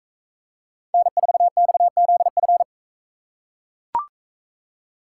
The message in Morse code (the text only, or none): N4XZF